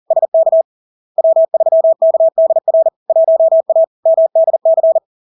Morse code, heard as a keyed tone, about 40 wpm, 650 Hz. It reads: SK W3KDR 1A MDC
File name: LARK-FD-2022-Log-in-CW-40wpm-650-Hz